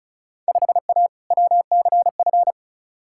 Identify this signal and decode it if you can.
Morse code, heard as a keyed tone, about 36 wpm, 700 Hz. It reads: 5A WCF
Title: Field Day Exchanges Encore - 35wpm